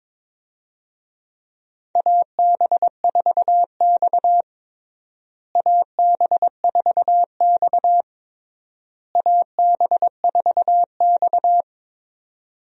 Morse code, heard as a keyed tone, about 21 wpm, 700 Hz. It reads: AB4X AB4X AB4X
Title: USA Call Signs - 4 Characters - Repeated 3x - 22wpm